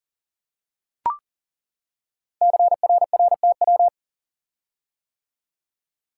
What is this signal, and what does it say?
Morse code, heard as a keyed tone, about 40 wpm, 700 Hz. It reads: CRRTW
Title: Code Groups - 5 Letters - 40wpm